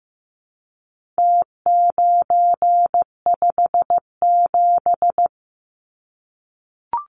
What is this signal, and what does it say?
Morse code, heard as a keyed tone, about 15 wpm, 700 Hz. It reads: T957
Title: Numbers - Contest Style - 4-Digit 15wpm